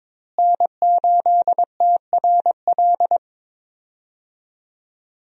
Morse code, heard as a keyed tone, about 23 wpm, 700 Hz. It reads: N8TRL
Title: USA Call Signs - 5 Characters - Repeated 2x - 22wpm